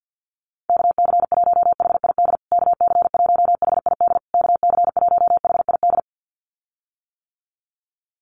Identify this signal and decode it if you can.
Morse code, heard as a keyed tone, about 50 wpm, 700 Hz. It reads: KC1HID KC1HID KC1HID
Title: USA Call Signs Repeated 3x 50wpm